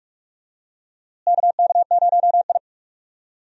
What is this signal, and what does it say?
Morse code, heard as a keyed tone, about 42 wpm, 700 Hz. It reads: KK0I
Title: USA Call Signs - 4 Characters - Repeated 3x - 45wpm